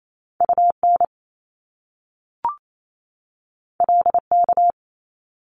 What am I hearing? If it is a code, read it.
Morse code, heard as a keyed tone, about 28 wpm, 700 Hz. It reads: UN LK
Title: Top Two Letter Combinations in Words 28wpm